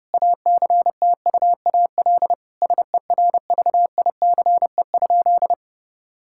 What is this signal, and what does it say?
Morse code, heard as a keyed tone, about 30 wpm, 700 Hz. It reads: ACTUAL SERVICE?
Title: Sentences from Top 1000 Words - Review Part 5 - 30wpm